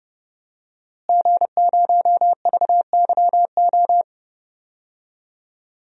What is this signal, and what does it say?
Morse code, heard as a keyed tone, about 30 wpm, 700 Hz. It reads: G0VYO
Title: DXCC-100 Call Signs - 5 Characters - Repeated 3x - 30wpm